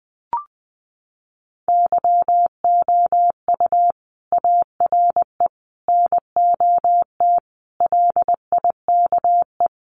Morse code, heard as a keyed tone, about 20 wpm, 700 Hz. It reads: YOU ARE NOT LIKE
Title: Sentences from Top 600 Words 20wpm